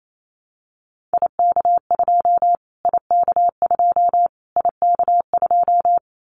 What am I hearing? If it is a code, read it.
Morse code, heard as a keyed tone, about 28 wpm, 700 Hz. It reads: IK2 IK2 IK2
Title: Callsign Prefixes for Top 25 DXCC - Repeated 3x - 28wpm